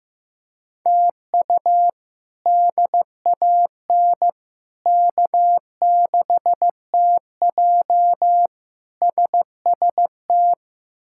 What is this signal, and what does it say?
Morse code, heard as a keyed tone, about 15 wpm, 700 Hz. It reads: TU DAN K6TJ SST
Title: K1USN SST QSO Practice - 15wpm